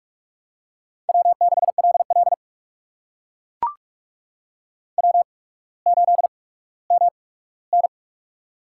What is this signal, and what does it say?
Morse code, heard as a keyed tone, about 44 wpm, spaced as for 19 wpm, 700 Hz. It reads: W6LL W8MN
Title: USA Call Signs - 4 Characters - 8x Character Spacing - 45wpm